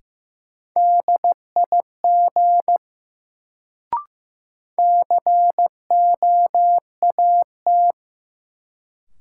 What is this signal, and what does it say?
Morse code, heard as a keyed tone, about 15 wpm, 700 Hz. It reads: DIG COAT
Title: Sentences from Top 1600 Words - New Word Drill - 15wpm